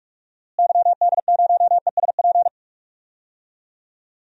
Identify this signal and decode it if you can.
Morse code, heard as a keyed tone, about 46 wpm, 700 Hz. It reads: YD0ESP
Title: DXCC-100 Call Signs - 6 Characters - Repeated 2x - 45wpm